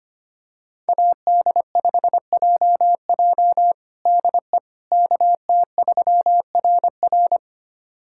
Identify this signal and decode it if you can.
Morse code, heard as a keyed tone, about 25 wpm, 700 Hz. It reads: AD5JJ DE KT3RR
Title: USA Callsign DE Callsign - 5 Characters - 25wpm